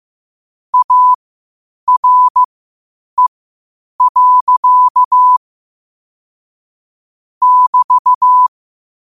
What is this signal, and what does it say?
Morse code, heard as a keyed tone, about 15 wpm, spaced as for 8 wpm, 1000 Hz. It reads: ARE. <BT>
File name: qotdfiles / 8wpm